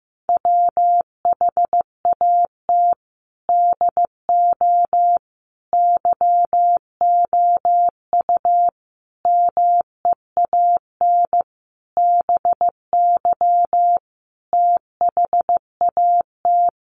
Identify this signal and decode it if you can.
Morse code, heard as a keyed tone, about 15 wpm, 700 Hz. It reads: WHAT DO YOU MEAN BY THAT
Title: Sentences from Top 1200 Words 15wpm